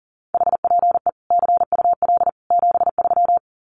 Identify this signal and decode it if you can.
Morse code, heard as a keyed tone, about 40 wpm, 700 Hz. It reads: HPE CUL 73
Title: Common QSO Phrases 40wpm